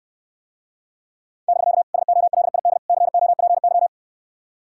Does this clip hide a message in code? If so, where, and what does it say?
Yes: Morse code from 1.5 to 3.9 s, heard as a keyed tone, about 69 wpm, 700 Hz. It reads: <BK> SOLID COPY